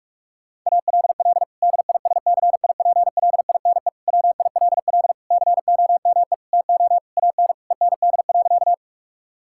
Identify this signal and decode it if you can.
Morse code, heard as a keyed tone, about 46 wpm, 700 Hz. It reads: ALL DISCIPLINE WILL COME TO AN END.
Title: Sentences from Top 1000 Words - Review Part 6 - 45wpm